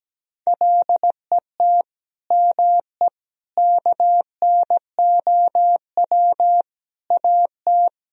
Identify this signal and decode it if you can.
Morse code, heard as a keyed tone, about 17 wpm, 700 Hz. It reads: LET ME KNOW AT
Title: Sentences from Top 300 Words and Abbreviations 17wpm